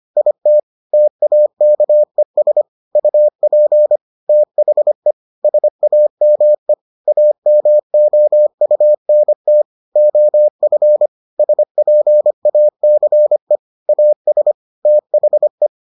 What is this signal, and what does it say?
Morse code, heard as a keyed tone, about 25 wpm, 600 Hz. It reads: IT TAKES UP THE SAME AMOUNT OF SPACE AS THE